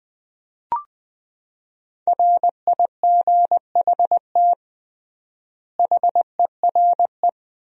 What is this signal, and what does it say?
Morse code, heard as a keyed tone, about 20 wpm, 700 Hz. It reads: RIGHT HERE
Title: 3x Word-Spacing - Sentences from Top 300 Words - 20wpm